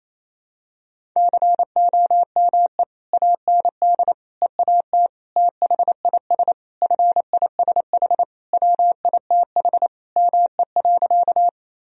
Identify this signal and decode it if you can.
Morse code, heard as a keyed tone, about 28 wpm, 700 Hz. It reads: COME AND EAT THIS FISH WITH ME.
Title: Sentences from Top 1000 Words - Review Part 1 - 28wpm